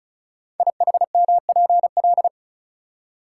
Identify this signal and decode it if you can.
Morse code, heard as a keyed tone, about 36 wpm, 700 Hz. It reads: IHMPL
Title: Code Groups - 5 Letters - 35wpm